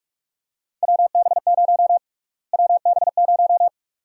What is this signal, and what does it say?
Morse code, heard as a keyed tone, about 45 wpm, 700 Hz. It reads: WB0 WB0
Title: USA Call Sign Prefixes - Repeated 2x - 45wpm